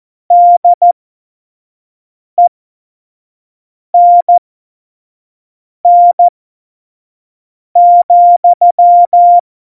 Morse code, heard as a keyed tone, about 14 wpm, spaced as for 5 wpm, 700 Hz. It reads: DENN,
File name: Morsekurs / Witze / Witz 07. November 2017, 25 F70